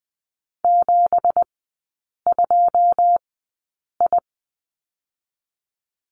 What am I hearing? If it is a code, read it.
Morse code, heard as a keyed tone, about 20 wpm, spaced as for 8 wpm, 700 Hz. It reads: 72I
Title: 184 Lesson 30e - Letter b - Three Characters 20wpm